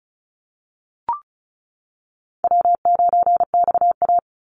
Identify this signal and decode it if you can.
Morse code, heard as a keyed tone, about 34 wpm, 700 Hz. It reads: W9XA
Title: Call Signs - 35wpm